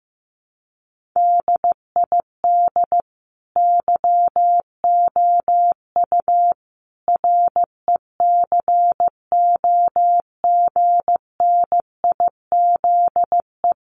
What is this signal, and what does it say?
Morse code, heard as a keyed tone, about 15 wpm, 700 Hz. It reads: DID YOU RECOGNIZE
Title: Sentences from Top 1400 Words 15wpm